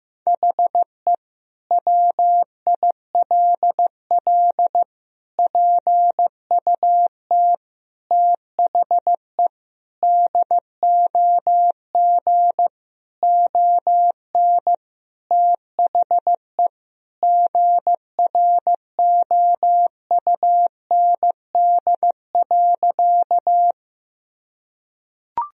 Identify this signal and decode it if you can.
Morse code, heard as a keyed tone, about 15 wpm, 700 Hz. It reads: HE WILL PUT THE DOG ON THE GROUND.
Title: Sentences from Top 1000 Words - Review Part 2 - 15wpm